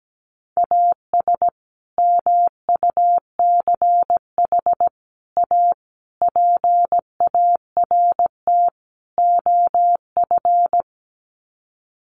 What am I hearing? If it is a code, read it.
Morse code, heard as a keyed tone, about 17 wpm, 700 Hz. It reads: AS MUCH A PART OF
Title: Sets of 5 Words Encore2 17wpm